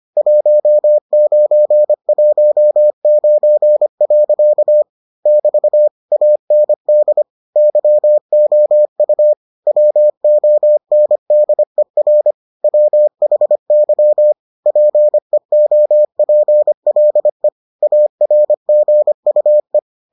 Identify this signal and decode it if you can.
Morse code, heard as a keyed tone, about 25 wpm, 600 Hz. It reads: 1919. <BT> AND YOU WONDER WHY PEOPLE ARGUE